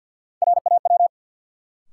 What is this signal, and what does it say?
Morse code, heard as a keyed tone, about 50 wpm, 700 Hz. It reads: RAW
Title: Sentences from Top 1500 Words - New Word Drill - 50wpm